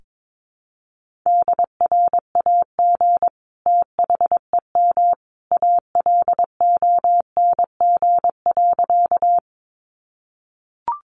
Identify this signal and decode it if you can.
Morse code, heard as a keyed tone, about 22 wpm, 700 Hz. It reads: DRAG THEM ALONG.
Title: Sentences from Top 1500 Words 22wpm